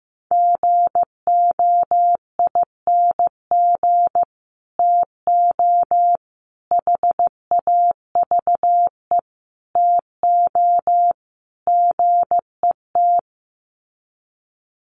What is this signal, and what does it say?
Morse code, heard as a keyed tone, about 15 wpm, 700 Hz. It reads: GOING TO HAVE TO GET
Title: Sets of 5 Words Encore2 15wpm